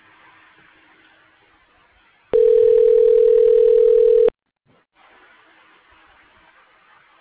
Ringback tone (North America), 440 + 480 Hz.